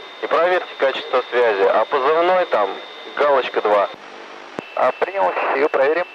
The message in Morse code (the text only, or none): none